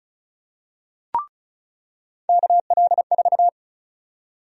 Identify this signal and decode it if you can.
Morse code, heard as a keyed tone, about 36 wpm, 700 Hz. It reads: KL4
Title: USA Call Sign Prefixes - 35wpm